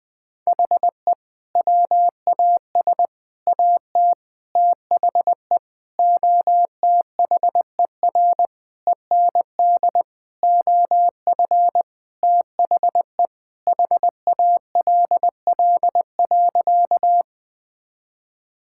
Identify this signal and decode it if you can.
Morse code, heard as a keyed tone, about 20 wpm, 700 Hz. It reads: HE WAS AT THE OTHER END OF THE HALL.
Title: Sentences with 5-Word Phrases - 20wpm